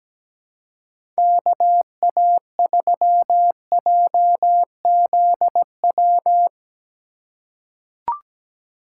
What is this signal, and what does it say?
Morse code, heard as a keyed tone, about 17 wpm, 700 Hz. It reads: KA3JZW
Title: USA Call Signs - 6 Characters - 3x Character Spacing - 17wpm